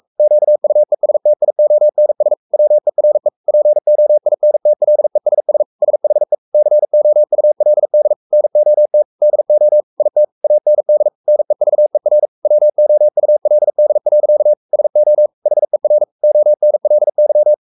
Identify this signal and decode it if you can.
Morse code, heard as a keyed tone, about 44 wpm, 600 Hz. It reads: QUESTIONS WERE POINTLESS SHE COULD NOT DO IT AND NEVER WOULD. SO HER ONLY
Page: Romanen Never af Ken Follet på engelsk.